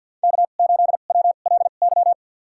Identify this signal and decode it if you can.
Morse code, heard as a keyed tone, about 50 wpm, spaced as for 36 wpm, 700 Hz. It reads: K8WLY
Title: USA Call Signs - 5 Characters - 2x Character Spacing - 50wpm